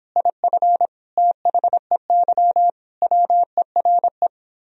Morse code, heard as a keyed tone, about 29 wpm, 700 Hz. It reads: IF THEY WERE